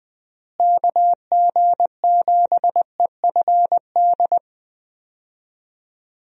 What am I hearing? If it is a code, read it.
Morse code, heard as a keyed tone, about 20 wpm, 700 Hz. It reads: KG7EFD
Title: USA Call Signs - 6 Characters - Repeated 3x - 20wpm